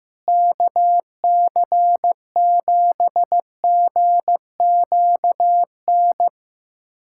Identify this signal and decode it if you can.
Morse code, heard as a keyed tone, about 15 wpm, 700 Hz. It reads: KC7GQN
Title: USA Callsign DE Callsign 15wpm